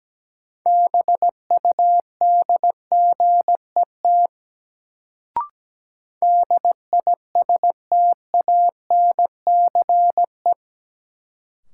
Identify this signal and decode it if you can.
Morse code, heard as a keyed tone, about 17 wpm, 700 Hz. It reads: BUDGET DISTANCE
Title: Sentences from Top 1200 Words - New Word Drill Alt - 17wpm